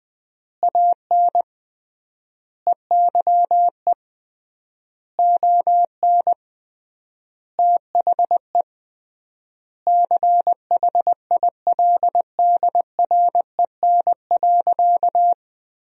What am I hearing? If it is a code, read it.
Morse code, heard as a keyed tone, about 20 wpm, 700 Hz. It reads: AN EYE ON THE CHILDREN.
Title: 3x Word-Spacing - Sentences from Top 300 Words - 20wpm